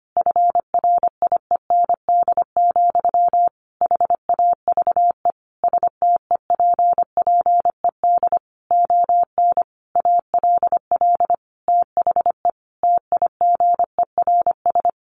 Morse code, heard as a keyed tone, about 25 wpm, 700 Hz. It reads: FRIEND, HAVE STEPPED ON ALL THE TIGERS
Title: A Sound Of Thunder 25wpm